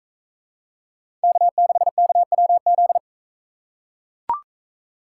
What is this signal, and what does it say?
Morse code, heard as a keyed tone, about 43 wpm, 700 Hz. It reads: K6KWZ